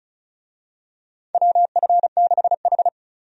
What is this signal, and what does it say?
Morse code, heard as a keyed tone, about 36 wpm, 700 Hz. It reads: WF6H
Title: USA Call Signs - 4 Characters - 35wpm